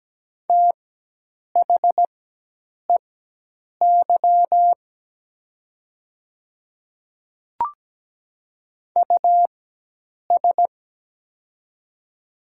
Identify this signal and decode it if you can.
Morse code, heard as a keyed tone, about 17 wpm, spaced as for 8 wpm, 700 Hz. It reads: THEY US
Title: 4x Character Spacing - Top 500 Words - 17wpm